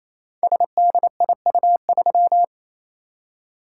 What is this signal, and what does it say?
Morse code, heard as a keyed tone, about 29 wpm, 700 Hz. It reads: SDIU3
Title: Code Groups - 5 Letter:Numbers - 28wpm